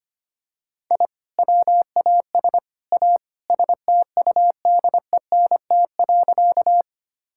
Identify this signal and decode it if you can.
Morse code, heard as a keyed tone, about 25 wpm, 700 Hz. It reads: I WAS A STUDENT.
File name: practice / 25wpm / Sentences with 3-Word Phrases - 25wpm